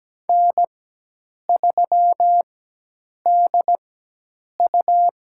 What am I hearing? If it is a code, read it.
Morse code, heard as a keyed tone, about 17 wpm, spaced as for 8 wpm, 700 Hz. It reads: N3DU
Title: USA Call Signs - 4x Character Spacing - 17wpm